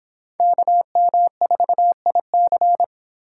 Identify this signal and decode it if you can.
Morse code, heard as a keyed tone, about 23 wpm, 700 Hz. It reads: KM4IC